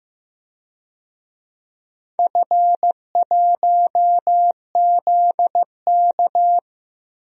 Morse code, heard as a keyed tone, about 15 wpm, 700 Hz. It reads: F1ZK
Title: DXCC-100 Call Signs - 4 Characters - 15wpm